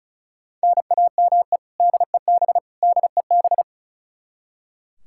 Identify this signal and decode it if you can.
Morse code, heard as a keyed tone, about 34 wpm, 700 Hz. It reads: NAME DEB DEB
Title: Practice QSO - Exchange 1 - Beginning - 35wpm